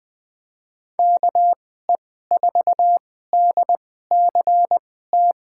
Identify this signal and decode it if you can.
Morse code, heard as a keyed tone, about 20 wpm, spaced as for 14 wpm, 700 Hz. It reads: KE4DCT
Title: USA Call Signs - 6 Characters - 2x Character Spacing - 20wpm